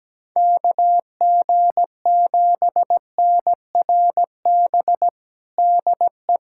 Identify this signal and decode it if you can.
Morse code, heard as a keyed tone, about 17 wpm, 700 Hz. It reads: KG7NRB DE
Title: USA Callsign DE Callsign 17wpm